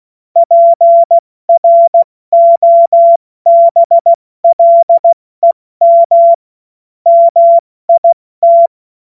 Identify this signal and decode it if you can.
Morse code, heard as a keyed tone, about 16 wpm, 680 Hz. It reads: PROBLEM MIT
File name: Morsekurs / Witze / Witz 16. Februar 2021, 70 F80